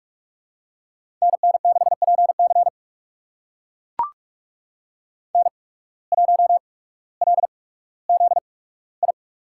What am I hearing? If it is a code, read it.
Morse code, heard as a keyed tone, about 43 wpm, spaced as for 19 wpm, 700 Hz. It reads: NN6PC N1LZI
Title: USA Call Signs - 8x Character Spacing - 45wpm